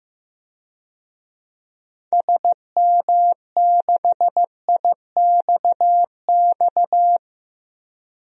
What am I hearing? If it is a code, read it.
Morse code, heard as a keyed tone, about 15 wpm, 700 Hz. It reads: SM6IXX
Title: DXCC-100 Call Signs - 6 Characters - 15wpm